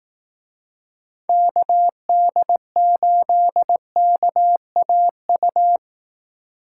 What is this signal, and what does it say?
Morse code, heard as a keyed tone, about 18 wpm, 700 Hz. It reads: KD8KAU
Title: Speed Racing - USA 2x3 Call Signs 15wpm